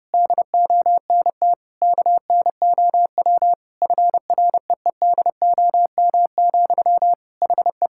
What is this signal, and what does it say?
Morse code, heard as a keyed tone, about 30 wpm, 700 Hz. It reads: DONT KNOW FREEDOM, HE
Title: Sentences from Top 1000 Words - Review Part 5 - 30wpm